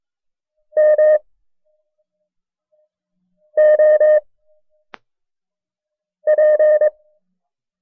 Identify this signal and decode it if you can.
Morse code, heard as a keyed tone, about 22 wpm, 610 Hz. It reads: M O P
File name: FreeFiles / CW-practice / KB6MT-tape1-side2